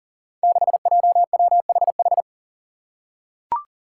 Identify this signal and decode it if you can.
Morse code, heard as a keyed tone, about 40 wpm, 700 Hz. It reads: 6JWHH
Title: Five Challenging Letter-Number Combinations 40wpm